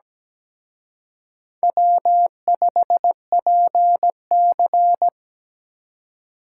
Morse code, heard as a keyed tone, about 20 wpm, 700 Hz. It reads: W5PC
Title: Speed Racing - USA 1x2 Call Signs 17wpm